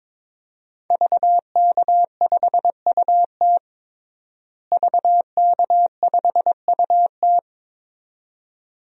Morse code, heard as a keyed tone, about 21 wpm, 700 Hz. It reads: VK5UT VK5UT
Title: DXCC-100 Call Signs - 5 Characters - Repeated 2x - 22wpm